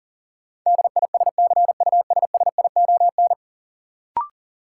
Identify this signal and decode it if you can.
Morse code, heard as a keyed tone, about 40 wpm, 700 Hz. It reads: DISCUSSION
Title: Sentences from Top 1100 Words - New Word Drill - 40wpm